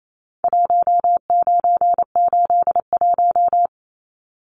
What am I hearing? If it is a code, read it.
Morse code, heard as a keyed tone, about 28 wpm, 700 Hz. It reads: 1981
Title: Numbers - Contest Style - 4-Digit 28wpm